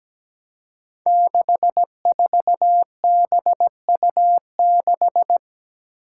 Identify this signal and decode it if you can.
Morse code, heard as a keyed tone, about 17 wpm, 700 Hz. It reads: 64BU6
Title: Five Challenging Letter-Number Combinations 17wpm